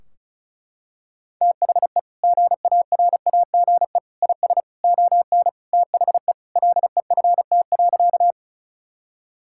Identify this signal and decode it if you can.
Morse code, heard as a keyed tone, about 36 wpm, 700 Hz. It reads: THE GARAGE IS ON THE LEFT.
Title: Sentences from Top 1600 Words 35wpm